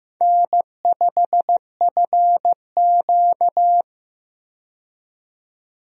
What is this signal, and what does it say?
Morse code, heard as a keyed tone, about 15 wpm, 700 Hz. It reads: N5FQ
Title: USA Call Signs - 4 Characters - 15wpm